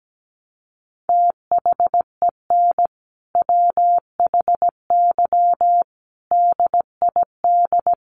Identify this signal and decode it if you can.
Morse code, heard as a keyed tone, about 17 wpm, 700 Hz. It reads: THEN WHY DID
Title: Sentences from Top 800 Words 17wpm